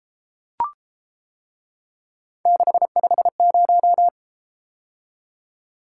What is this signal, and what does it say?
Morse code, heard as a keyed tone, about 31 wpm, 700 Hz. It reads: 650